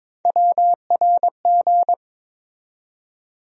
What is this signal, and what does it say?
Morse code, heard as a keyed tone, about 22 wpm, 700 Hz. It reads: WRG
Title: USA Call Sign Postfixes - 3 Letters 22wpm